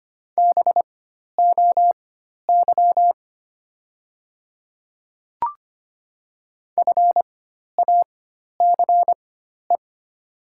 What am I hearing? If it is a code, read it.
Morse code, heard as a keyed tone, about 25 wpm, spaced as for 12 wpm, 700 Hz. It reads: BOY FACE
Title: 4x Character Spacing - Top 500 Words - 25wpm